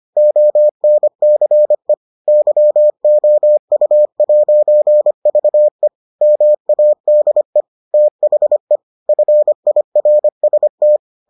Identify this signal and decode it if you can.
Morse code, heard as a keyed tone, about 25 wpm, 600 Hz. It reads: ONCE YOU'VE MADE THE FIRST